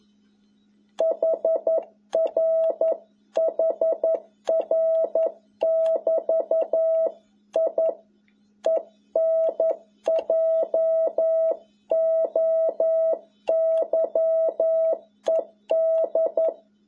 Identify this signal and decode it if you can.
Morse code, heard as a keyed tone, about 11 wpm, 650 Hz. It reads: HR HR <BT> I ENJOYED
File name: jh8cbh / 2021cw